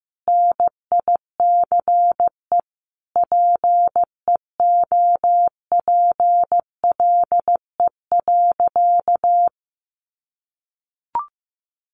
Morse code, heard as a keyed tone, about 15 wpm, 700 Hz. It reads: NICE PEOPLE.
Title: Sentences from Top 1100 Words 15wpm